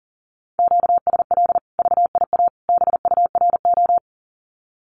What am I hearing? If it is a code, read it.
Morse code, heard as a keyed tone, about 38 wpm, 700 Hz. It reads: QSL VIA BURO
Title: Common QSO Phrases 40wpm